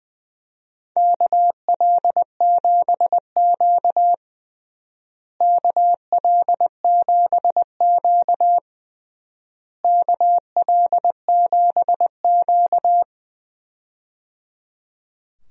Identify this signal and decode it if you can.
Morse code, heard as a keyed tone, about 21 wpm, 700 Hz. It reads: KL7Q KL7Q KL7Q
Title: DXCC-100 Call Signs - 4 Characters - Repeated 3x - 20wpm